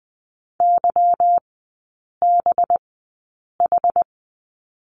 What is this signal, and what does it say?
Morse code, heard as a keyed tone, about 20 wpm, spaced as for 8 wpm, 700 Hz. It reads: YBH
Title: 184 Lesson 30e - Letter b - Three Characters 20wpm